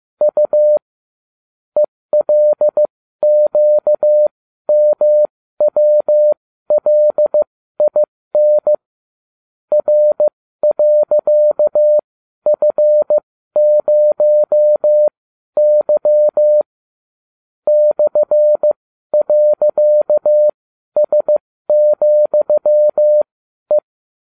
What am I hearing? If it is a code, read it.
Morse code, heard as a keyed tone, about 15 wpm, spaced as for 12 wpm, 600 Hz. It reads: U ELQMWLIN R.F0Y /.S,E